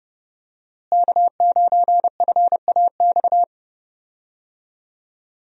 Morse code, heard as a keyed tone, about 30 wpm, 700 Hz. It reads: K9FAX
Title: USA Call Signs - 5 Characters - Repeated 3x - 30wpm